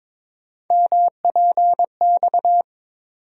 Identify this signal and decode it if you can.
Morse code, heard as a keyed tone, about 22 wpm, 700 Hz. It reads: MPX
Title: USA Call Sign Postfixes - 3 Letters 22wpm